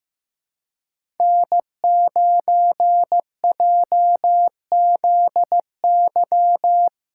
Morse code, heard as a keyed tone, about 15 wpm, 700 Hz. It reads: N9JZY
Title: K1USN SST QSO Practice - 15wpm